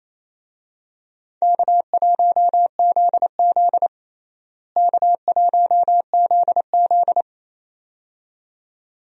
Morse code, heard as a keyed tone, about 28 wpm, 700 Hz. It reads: K1ZZ K1ZZ
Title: USA Call Signs - 4 Characters - Repeated 2x - 28wpm